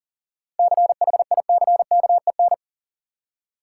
Morse code, heard as a keyed tone, about 40 wpm, 700 Hz. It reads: CHICKEN
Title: Seven Letter Words - 40wpm